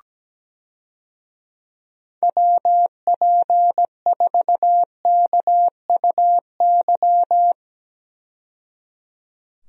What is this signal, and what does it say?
Morse code, heard as a keyed tone, about 17 wpm, 700 Hz. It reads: WP4KUY
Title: USA Call Signs - 6 Characters - 17wpm